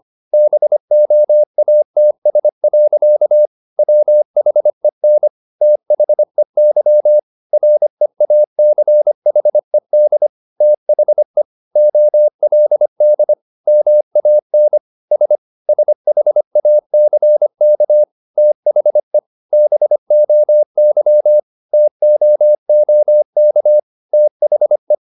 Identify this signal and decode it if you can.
Morse code, heard as a keyed tone, about 25 wpm, 600 Hz. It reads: BOATS. WHEN THEY REACHED THE OLD MAN S SHACK THE BOY TOOK THE
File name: the old man and the sea part 1 25wpm